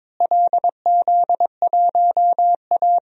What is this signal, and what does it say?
Morse code, heard as a keyed tone, about 22 wpm, 700 Hz. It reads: LZ1A
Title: DXCC-100 Call Signs - 4 Characters - Repeated 3x - 22wpm